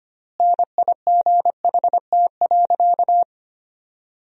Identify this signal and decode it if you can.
Morse code, heard as a keyed tone, about 25 wpm, 700 Hz. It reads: NIGHT.
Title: 4x Word-Spacing - Sentences from Top 300 Words - 25wpm